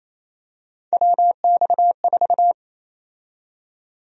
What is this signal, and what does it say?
Morse code, heard as a keyed tone, about 28 wpm, 700 Hz. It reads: WX4
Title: USA Call Sign Prefixes - 28wpm